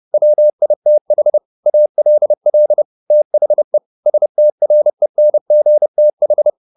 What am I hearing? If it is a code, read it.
Morse code, heard as a keyed tone, about 30 wpm, 600 Hz. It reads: WITH ALL THE STRENGTH
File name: the old man and the sea / the old man and the sea 1 30wpm